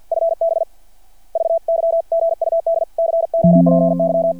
Morse code, heard as a keyed tone, about 50 wpm, 720 Hz. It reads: FB VY GUD COPY
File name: archived / CRSnet